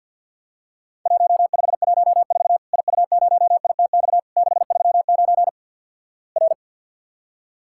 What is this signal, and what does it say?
Morse code, heard as a keyed tone, about 47 wpm, 700 Hz. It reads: 1514 IU0ITX 639 R
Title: Worked All Europe DX Contest - QTC Exchange with Repeats - 50wpm